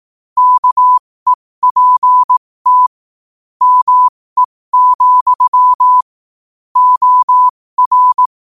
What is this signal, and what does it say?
Morse code, heard as a keyed tone, about 18 wpm, spaced as for 15 wpm, 1000 Hz. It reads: KEPT ME, OR
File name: qotdfiles / 15wpm